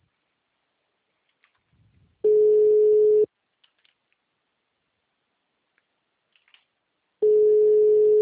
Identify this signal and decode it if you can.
Ringback tone (Europe, 425 Hz).